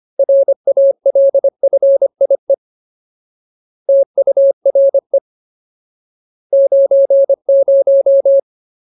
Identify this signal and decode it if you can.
Morse code, heard as a keyed tone, about 25 wpm, 550 Hz. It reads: RALFIE TURE 90
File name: Ultimate Word List / 25WPM Ultimate